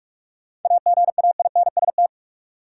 Morse code, heard as a keyed tone, about 47 wpm, 700 Hz. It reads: AGAINST
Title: Sentences from Top 400 Words - New Word Drill - 45wpm